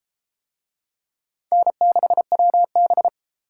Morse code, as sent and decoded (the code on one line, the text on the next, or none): -. -.... .-- -...
N6WB